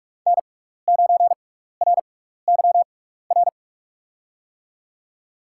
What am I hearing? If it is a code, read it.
Morse code, heard as a keyed tone, about 47 wpm, spaced as for 16 wpm, 700 Hz. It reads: N9RYR
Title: USA Call Signs - 6x Character Spacing - 45wpm